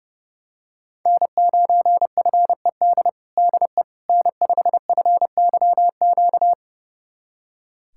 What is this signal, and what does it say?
Morse code, heard as a keyed tone, about 31 wpm, 700 Hz. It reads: N9FED DE N5FYQ
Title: USA Callsign DE Callsign - 5 Characters - 30wpm